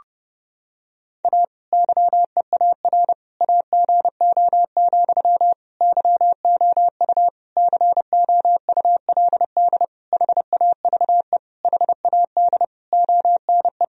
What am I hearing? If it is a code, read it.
Morse code, heard as a keyed tone, about 30 wpm, 700 Hz. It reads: A YEAR AGO, YOU COULD HAVE HAD ONE
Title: Sentences from Top 1000 Words - Review Part 3 - 30wpm